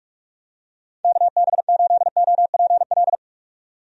Morse code, heard as a keyed tone, about 42 wpm, 700 Hz. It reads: KB8OPL
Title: USA Call Signs - 6 Characters - 45wpm